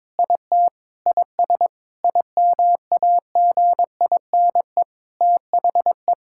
Morse code, heard as a keyed tone, about 22 wpm, 700 Hz. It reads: IT IS IMAGINE THE
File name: Sentences with 7-Word Phrases - 22wpm